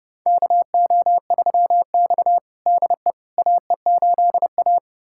Morse code, heard as a keyed tone, about 30 wpm, 700 Hz. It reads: KO3X DE AE8A
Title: USA Callsign DE Callsign - 4 Characters - 30wpm